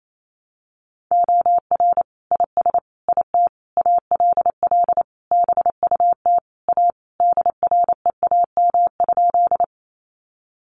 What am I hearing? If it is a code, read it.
Morse code, heard as a keyed tone, about 28 wpm, 700 Hz. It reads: OR IS IT ALL BUT A DREAM?
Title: Sentences from Top 1500 Words 28wpm